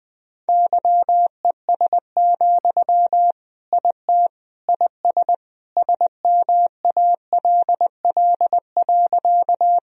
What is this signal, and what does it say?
Morse code, heard as a keyed tone, about 21 wpm, 700 Hz. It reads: YES, IT IS SMALL.
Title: Sentences from Top 1000 Words - Review Part 1 - 20wpm